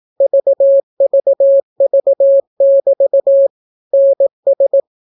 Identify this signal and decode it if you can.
Morse code, heard as a keyed tone, about 18 wpm, 550 Hz. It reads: VVV<BT> NS